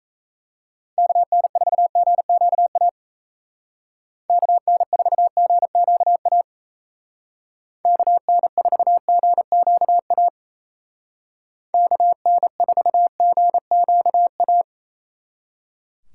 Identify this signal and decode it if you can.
Morse code, heard as a keyed tone, about 35 wpm, 700 Hz. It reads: KN4GQA KN4GQA KN4GQA KN4GQA
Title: Speed Racing - USA 2x3 Call Signs 28wpm